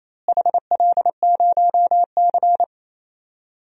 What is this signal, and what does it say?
Morse code, heard as a keyed tone, about 27 wpm, 700 Hz. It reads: HL0C
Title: DXCC-100 Call Signs - 4 Characters - Repeated 3x - 28wpm